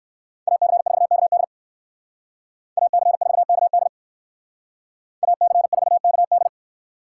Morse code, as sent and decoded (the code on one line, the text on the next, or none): .- -.-. ....- -.- -.. / .- -.-. ....- -.- -.. / .- -.-. ....- -.- -..
AC4KD AC4KD AC4KD